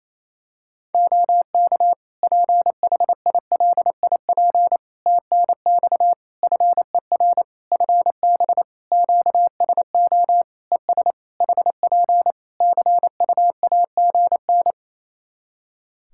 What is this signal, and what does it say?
Morse code, heard as a keyed tone, about 28 wpm, 700 Hz. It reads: OK PHILIP TNX FER FB QSO ES HP CUAGN
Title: K1USN SST QSO Practice 28wpm